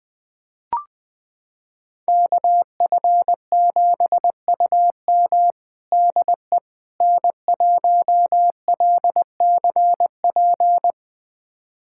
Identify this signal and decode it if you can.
Morse code, heard as a keyed tone, about 21 wpm, 700 Hz. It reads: KF7UM DE N1LCP
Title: USA Callsign DE Callsign - 5 Characters - 20wpm